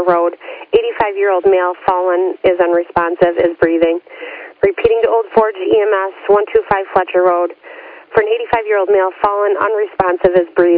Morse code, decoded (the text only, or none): none